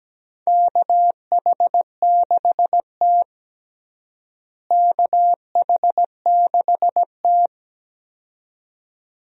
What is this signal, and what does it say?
Morse code, heard as a keyed tone, about 17 wpm, 700 Hz. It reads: KH6T KH6T
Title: DXCC-100 Call Signs - 4 Characters - Repeated 2x - 17wpm